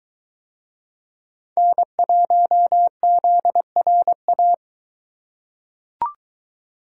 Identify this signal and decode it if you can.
Morse code, heard as a keyed tone, about 21 wpm, 700 Hz. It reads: N1ZRA